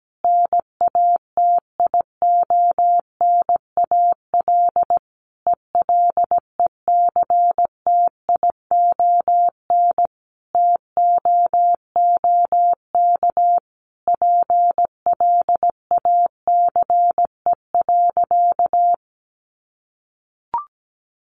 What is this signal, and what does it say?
Morse code, heard as a keyed tone, about 17 wpm, 700 Hz. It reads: NATIONAL ELECTION TOOK PLACE.
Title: Sentences from Top 1500 Words 17wpm